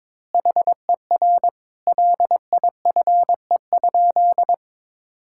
Morse code, heard as a keyed tone, about 21 wpm, 700 Hz. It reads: HER LIFE?
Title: Sentences from Top 1000 Words - Review Part 6 - 22wpm